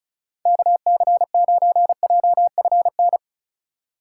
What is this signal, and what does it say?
Morse code, heard as a keyed tone, about 36 wpm, 700 Hz. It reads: KC9JFN
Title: USA Call Signs - 6 Characters - 6x Character Spacing - 35wpm